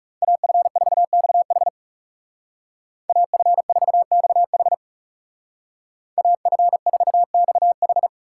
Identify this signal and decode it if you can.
Morse code, heard as a keyed tone, about 39 wpm, 700 Hz. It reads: AF4XH AF4XH AF4XH